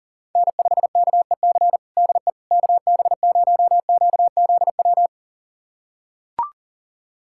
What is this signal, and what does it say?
Morse code, heard as a keyed tone, about 40 wpm, 700 Hz. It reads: N5KEC DE KB0QZW
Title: USA Callsign DE Callsign 40wpm